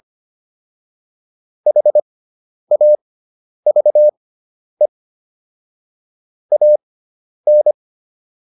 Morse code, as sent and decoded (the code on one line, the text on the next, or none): .... .- ...- . / .- -.
HAVE AN